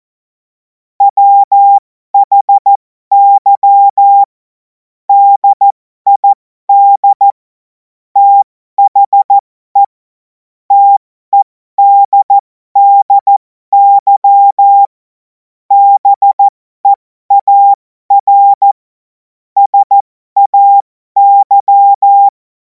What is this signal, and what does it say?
Morse code, heard as a keyed tone, about 14 wpm, spaced as for 12 wpm, 800 Hz. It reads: WHY DID THE TEDDY BEAR SAY
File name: Morsekurs / Witze